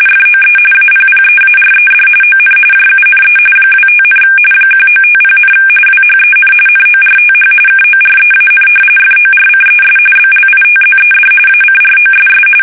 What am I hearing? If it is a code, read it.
RTTY, about 45.45 baud. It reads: FOE APPEARED ON JUNE 25, 1950, WHEN NORTH KOREAN FORCES BLITZED ACROSS
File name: fsk850 / battleshipnj